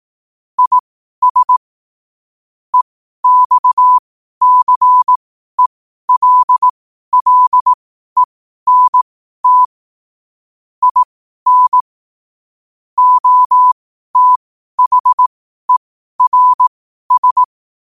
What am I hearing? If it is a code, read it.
Morse code, heard as a keyed tone, about 18 wpm, spaced as for 12 wpm, 1000 Hz. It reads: IS EXCELLENT IN OTHERS
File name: qotdfiles / 12wpm